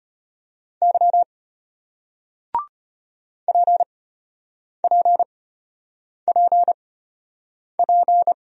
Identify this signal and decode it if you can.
Morse code, heard as a keyed tone, about 33 wpm, spaced as for 8 wpm, 700 Hz. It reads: Y PPPP